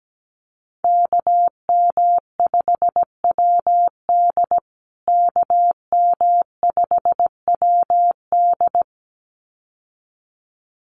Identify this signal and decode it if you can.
Morse code, heard as a keyed tone, about 17 wpm, 700 Hz. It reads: KM5WD KM5WD
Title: USA Call Signs Repeated 2x 17wpm